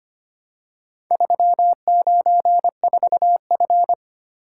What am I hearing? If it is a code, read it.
Morse code, heard as a keyed tone, about 25 wpm, 700 Hz. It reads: 394F
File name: practice / 25wpm / Four Letter-Number Combinations - 25wpm